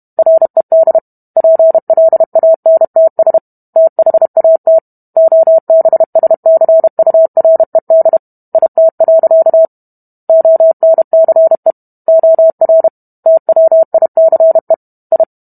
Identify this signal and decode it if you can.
Morse code, heard as a keyed tone, about 32 wpm, 650 Hz. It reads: RED PLANTS THAT OBSCURED IT. ONCE OR TWICE I